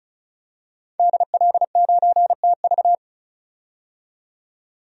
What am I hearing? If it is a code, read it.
Morse code, heard as a keyed tone, about 34 wpm, 700 Hz. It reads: DL9TV
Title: DXCC-100 Call Signs - 5 Characters - 35wpm